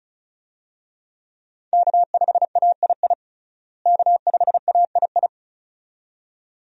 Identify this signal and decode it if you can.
Morse code, heard as a keyed tone, about 36 wpm, 700 Hz. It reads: K5AII K5AII
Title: USA Call Signs - 5 Characters - Repeated 2x - 35wpm